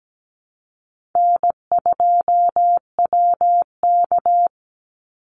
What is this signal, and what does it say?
Morse code, heard as a keyed tone, about 17 wpm, 700 Hz. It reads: N2WK
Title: Easy Call Signs Repeated 2x - 17wpm